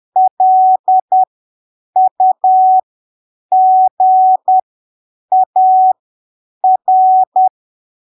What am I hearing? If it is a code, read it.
Morse code, heard as a keyed tone, about 10 wpm, spaced as for 7 wpm, 750 Hz. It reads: LUGAR
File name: arquivos / texto / texto 28 10ppm